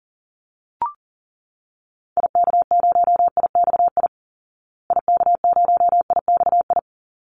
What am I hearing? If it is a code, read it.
Morse code, heard as a keyed tone, about 40 wpm, 700 Hz. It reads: IK0IXI IK0IXI
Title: Call Signs Repeated 3x - 40wpm